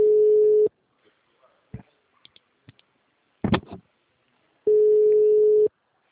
Ringback tone (Europe, 425 Hz).
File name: out- -2018- - - .21954